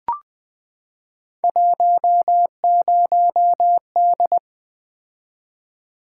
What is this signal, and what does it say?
Morse code, heard as a keyed tone, about 20 wpm, 700 Hz. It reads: 10D
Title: Three Letter-Number Combinations - 20wpm